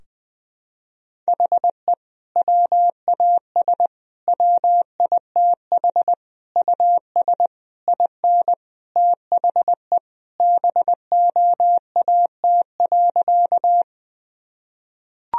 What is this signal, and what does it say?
Morse code, heard as a keyed tone, about 20 wpm, 700 Hz. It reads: HE WAS WITH US IN THE BOAT.
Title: Sentences from Top 1000 Words - Review Part 3 - 20wpm